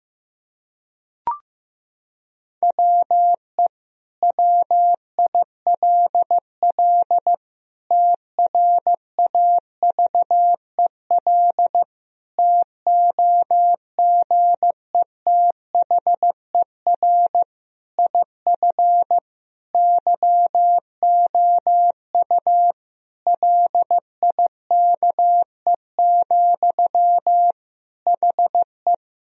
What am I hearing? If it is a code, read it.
Morse code, heard as a keyed tone, about 15 wpm, 700 Hz. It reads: WE WILL TRAVEL TOGETHER IF YOU LIKE, HE
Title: Sentences from Top 1000 Words - Review Part 2 - 15wpm